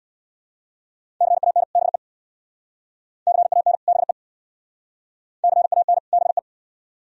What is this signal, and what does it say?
Morse code, heard as a keyed tone, about 61 wpm, 700 Hz. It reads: CAN BE CAN BE CAN BE
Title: Speed-Racing - Sets of 2 Words 50wpm